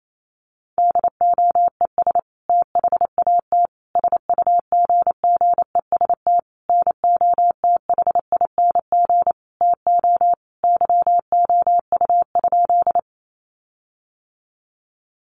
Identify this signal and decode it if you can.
Morse code, heard as a keyed tone, about 28 wpm, 700 Hz. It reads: DOES THAT SUGGEST NOTHING TO YOU?
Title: Sentences from Top 1300 Words 28wpm